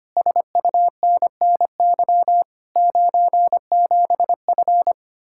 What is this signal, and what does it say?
Morse code, heard as a keyed tone, about 25 wpm, 700 Hz. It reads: SUNNY 97F
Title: Speed-Racing Common QSO Phrases - 25wpm